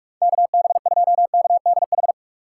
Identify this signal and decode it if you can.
Morse code, heard as a keyed tone, about 59 wpm, spaced as for 51 wpm, 700 Hz. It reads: KB2KDH
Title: Speed Racing - USA 2x3 Call Signs 45wpm